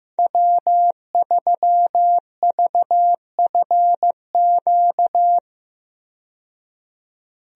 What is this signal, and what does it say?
Morse code, heard as a keyed tone, about 15 wpm, 700 Hz. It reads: W3VFQ
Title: USA Call Signs - 5 Characters - 15wpm